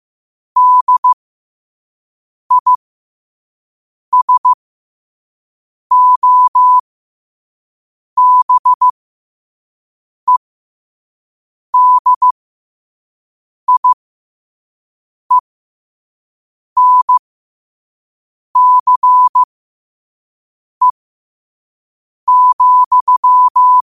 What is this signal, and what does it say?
Morse code, heard as a keyed tone, about 15 wpm, spaced as for 5 wpm, 1000 Hz. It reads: DISOBEDIENCE,
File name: qotdfiles / 5wpm